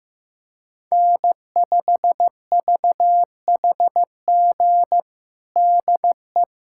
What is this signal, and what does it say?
Morse code, heard as a keyed tone, about 15 wpm, 700 Hz. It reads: N5VHG DE
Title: USA Callsign DE Callsign - 5 Characters - 15wpm